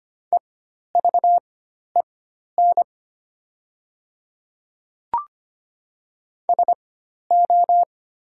Morse code, heard as a keyed tone, about 25 wpm, spaced as for 12 wpm, 700 Hz. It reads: EVEN SO
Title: 4x Character Spacing - Top 200 Words - 25wpm